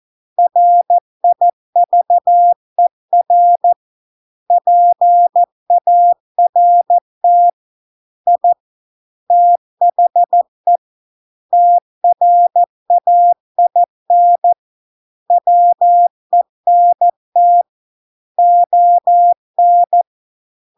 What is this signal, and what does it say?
Morse code, heard as a keyed tone, about 14 wpm, 700 Hz. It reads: RIVER PART I THE TRAIN WENT ON